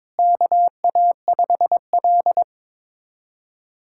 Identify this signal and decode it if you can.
Morse code, heard as a keyed tone, about 22 wpm, 700 Hz. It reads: KA5L
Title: USA Callsign DE Callsign - 4 Characters - 22wpm